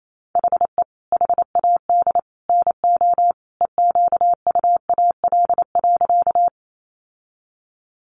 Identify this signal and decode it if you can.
Morse code, heard as a keyed tone, about 28 wpm, 700 Hz. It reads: HE HAD NO EQUAL.
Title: Sentences from Top 1400 Words 28wpm